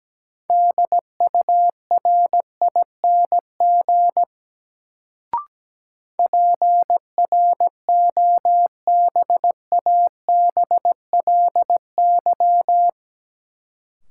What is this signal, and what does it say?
Morse code, heard as a keyed tone, about 17 wpm, 700 Hz. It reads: DURING PROBABLY
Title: Sentences from Top 700 Words - New Word Drill - 17wpm